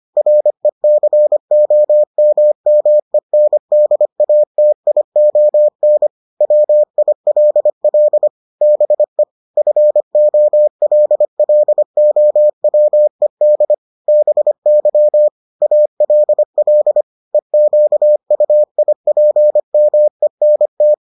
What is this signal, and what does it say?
Morse code, heard as a keyed tone, about 25 wpm, 600 Hz. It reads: RECOMMENDATION WILL BE FOLLOWED BY ALL EQUIPMENT